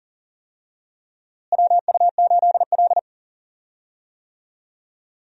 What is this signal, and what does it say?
Morse code, heard as a keyed tone, about 40 wpm, 700 Hz. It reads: WU8L
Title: USA Call Signs - 4 Characters - Repeated 3x - 40wpm